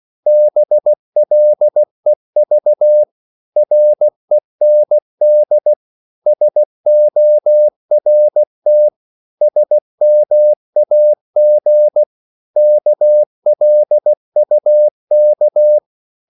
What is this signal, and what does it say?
Morse code, heard as a keyed tone, about 16 wpm, 600 Hz. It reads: BLEV REND SORT SMAG KLUK
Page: Korte danske ord 16wpm